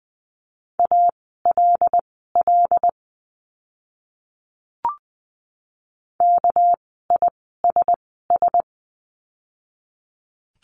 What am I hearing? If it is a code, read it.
Morse code, heard as a keyed tone, about 20 wpm, spaced as for 14 wpm, 700 Hz. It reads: ALL KISS
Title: 199 Lesson 32f - Letter k - Words 20wpm